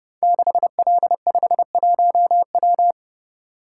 Morse code, heard as a keyed tone, about 30 wpm, 700 Hz. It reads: 6L51W
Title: Code Groups - 5 Letter:Numbers - 30wpm